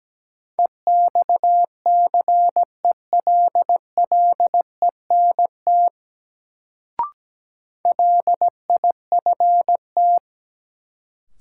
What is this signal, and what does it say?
Morse code, heard as a keyed tone, about 17 wpm, 700 Hz. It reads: EXCELLENT LIFT
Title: Sentences from Top 1000 Words - New Word Drill - 17wpm